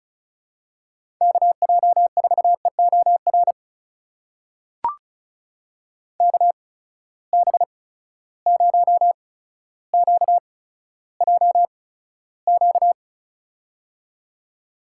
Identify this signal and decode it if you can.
Morse code, heard as a keyed tone, about 36 wpm, spaced as for 15 wpm, 700 Hz. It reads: KJ4EOR KB0QJQ
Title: USA Call Signs - 6 Characters - 8x Character Spacing - 35wpm